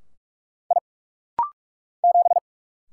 Morse code, heard as a keyed tone, about 44 wpm, 700 Hz. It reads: I 7
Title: Single Letter-Number - Rapid-Fire 45wpm